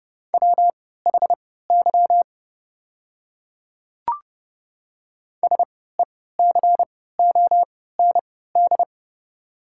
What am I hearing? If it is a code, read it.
Morse code, heard as a keyed tone, about 30 wpm, spaced as for 17 wpm, 700 Hz. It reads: WHY SECOND
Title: 3x Character Spacing - Top 500 Words - 30wpm